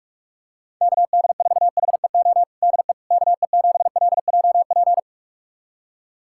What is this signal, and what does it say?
Morse code, heard as a keyed tone, about 41 wpm, 700 Hz. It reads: KD4HEO DE KE7LJP
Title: USA Callsign DE Callsign 45wpm